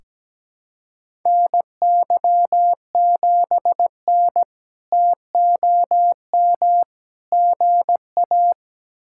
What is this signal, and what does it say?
Morse code, heard as a keyed tone, about 17 wpm, 700 Hz. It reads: NY7N TOM GA
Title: K1USN SST QSO Practice - 17wpm